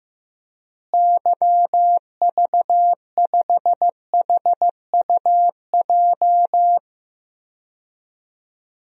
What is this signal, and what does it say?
Morse code, heard as a keyed tone, about 15 wpm, 700 Hz. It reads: YV5HUJ
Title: DXCC-100 Call Signs - 6 Characters - Repeated 2x - 15wpm